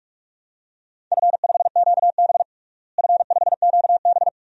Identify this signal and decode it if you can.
Morse code, heard as a keyed tone, about 42 wpm, 700 Hz. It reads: F5QB F5QB
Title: DXCC-100 Call Signs - 4 Characters - Repeated 3x - 45wpm